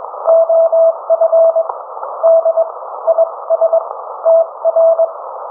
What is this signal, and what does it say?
Morse code, heard as a keyed tone, about 22 wpm, spaced as for 19 wpm, 660 Hz. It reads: OF D IS TR